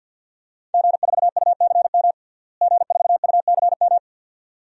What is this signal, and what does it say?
Morse code, heard as a keyed tone, about 51 wpm, 700 Hz. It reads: G4UCM G4UCM
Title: DXCC-100 Call Signs - 5 Characters - Repeated 2x - 50wpm